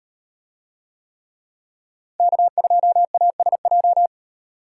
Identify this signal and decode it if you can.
Morse code, heard as a keyed tone, about 37 wpm, 700 Hz. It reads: K2ASJ